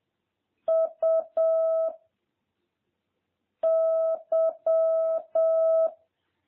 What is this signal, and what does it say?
Morse code, heard as a keyed tone, about 7 wpm, 650 Hz. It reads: U Y